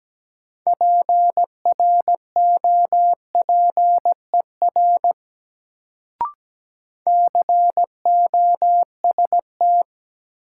Morse code, heard as a keyed tone, about 17 wpm, 700 Hz. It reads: PROPER COST
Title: Sentences from Top 800 Words - New Word Drill - 17wpm